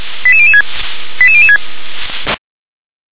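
Five-tone selective call (ZVEI), heard as quick runs of five tones, tones of about 70 ms. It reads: 88006 88006
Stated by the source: ZVEI five-tone selective call received on SDR.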